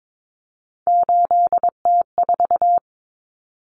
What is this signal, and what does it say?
Morse code, heard as a keyed tone, about 22 wpm, 700 Hz. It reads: 8T4
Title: Common Cut Numbers - 3-Digit 22wpm